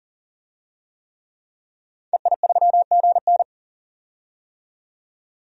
Morse code, heard as a keyed tone, about 40 wpm, 700 Hz. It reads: EI3GN
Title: DXCC-100 Call Signs - 5 Characters - 40wpm